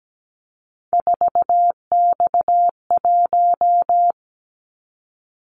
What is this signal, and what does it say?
Morse code, heard as a keyed tone, about 17 wpm, 700 Hz. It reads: 4X1
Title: Callsign Prefixes for Top 50 DXCC - 17wpm